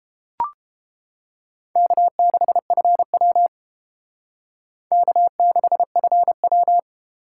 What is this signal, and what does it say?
Morse code, heard as a keyed tone, about 30 wpm, 700 Hz. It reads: K6FW K6FW